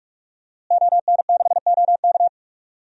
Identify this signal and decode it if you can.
Morse code, heard as a keyed tone, about 43 wpm, 700 Hz. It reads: ON6OK
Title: DXCC-100 Call Signs - 5 Characters - Repeated 2x - 45wpm